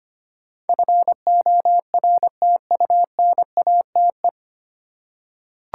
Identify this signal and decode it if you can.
Morse code, heard as a keyed tone, about 25 wpm, 690 Hz. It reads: FORTUNATE
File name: practice / 25wpm / 9 Letter Words - First 3 Letters Given - 25wpm